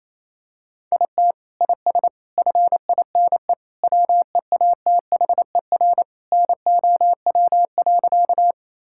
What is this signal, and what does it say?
Morse code, heard as a keyed tone, about 28 wpm, 700 Hz. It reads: IT IS FINE WEATHER NOW.
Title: Sentences from Top 1000 Words - Review Part 5 - 28wpm